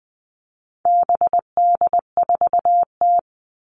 Morse code, heard as a keyed tone, about 20 wpm, 700 Hz. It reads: BD4T
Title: All Cut Numbers - 4-Digit 20wpm